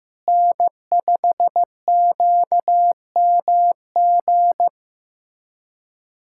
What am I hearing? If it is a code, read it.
Morse code, heard as a keyed tone, about 15 wpm, 700 Hz. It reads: N5QMG
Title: USA Callsign DE Callsign - 5 Characters - 15wpm